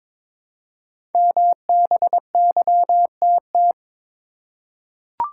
Morse code, heard as a keyed tone, about 22 wpm, 700 Hz. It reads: MBYTT
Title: Code Groups - 5 Letter:Numbers - 22wpm